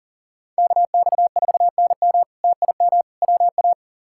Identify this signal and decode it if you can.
Morse code, heard as a keyed tone, about 40 wpm, 700 Hz. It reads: KX4NM TIM WA
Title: K1USN SST QSO Practice - 40wpm